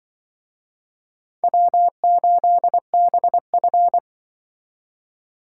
Morse code, heard as a keyed tone, about 24 wpm, 700 Hz. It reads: W8BF
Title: Speed Racing - USA 1x2 Call Signs 20wpm